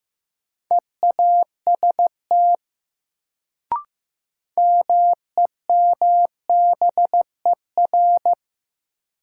Morse code, heard as a keyed tone, about 15 wpm, 700 Hz. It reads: EAST MEMBER
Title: Sets of 3 Words - New Word Drill - 15wpm